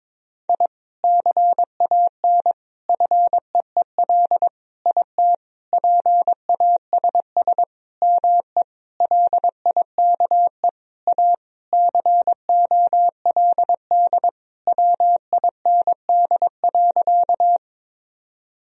Morse code, heard as a keyed tone, about 21 wpm, 700 Hz. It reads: I CAN FEEL IT PASS ME LIKE A COLD WIND.
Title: Sentences from Top 1000 Words - Review Part 2 - 22wpm